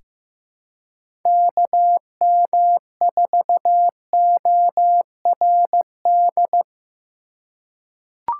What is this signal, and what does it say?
Morse code, heard as a keyed tone, about 15 wpm, 700 Hz. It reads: KM4ORD
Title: USA Call Signs - 4x Character Spacing - 15wpm